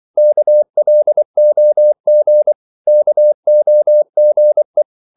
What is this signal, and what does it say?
Morse code, heard as a keyed tone, about 24 wpm, 600 Hz. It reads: KLOG KOGE
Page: Korte danske ord 24wpm